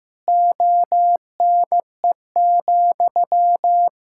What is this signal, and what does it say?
Morse code, heard as a keyed tone, about 16 wpm, 700 Hz. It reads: ONE,
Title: Sentences from Top 1000 Words - Review Part 5 - 15wpm